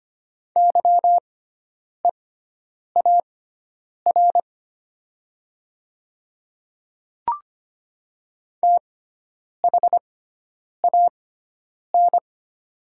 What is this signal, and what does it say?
Morse code, heard as a keyed tone, about 25 wpm, spaced as for 9 wpm, 700 Hz. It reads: YEAR THAN
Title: 6x Character Spacing - Top 300 Words - 25wpm